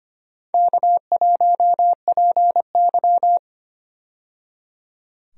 Morse code, heard as a keyed tone, about 25 wpm, 700 Hz. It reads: K1PY
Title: USA Call Signs - 4 Characters - 25wpm